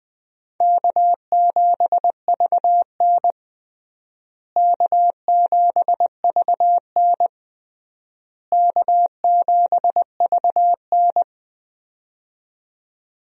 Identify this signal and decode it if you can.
Morse code, heard as a keyed tone, about 21 wpm, 700 Hz. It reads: K7VN K7VN K7VN
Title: USA Call Signs - 4 Characters - Repeated 3x - 20wpm